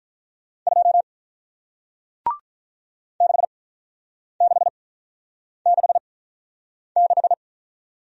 Morse code, heard as a keyed tone, about 47 wpm, spaced as for 9 wpm, 700 Hz. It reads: 2 6666